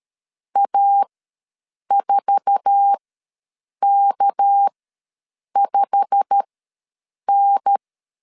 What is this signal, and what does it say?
Morse code, heard as a keyed tone, about 13 wpm, spaced as for 7 wpm, 800 Hz. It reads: A4K5N
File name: LEZIONI / 9 28 ws 12wpm 3354